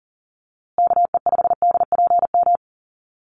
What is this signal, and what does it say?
Morse code, heard as a keyed tone, about 40 wpm, 700 Hz. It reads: KE5DPM
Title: USA Call Signs Repeated 2x 40wpm